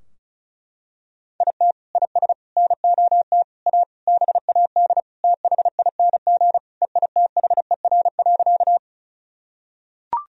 Morse code, heard as a keyed tone, about 34 wpm, 700 Hz. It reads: IT IS NOT A BAD THING EITHER.
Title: Sentences from Top 1000 Words - Review Part 4 - 35wpm